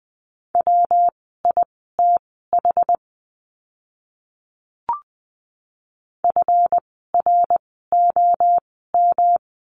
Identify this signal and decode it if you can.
Morse code, heard as a keyed tone, about 20 wpm, spaced as for 14 wpm, 700 Hz. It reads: WITH FROM
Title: 206 Lesson 33f - Letter j - Words 20wpm